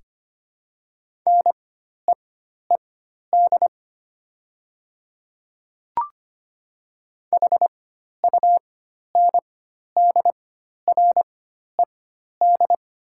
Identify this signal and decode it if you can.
Morse code, heard as a keyed tone, about 25 wpm, spaced as for 12 wpm, 700 Hz. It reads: NEED HUNDRED
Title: 4x Character Spacing - Top 500 Words - 25wpm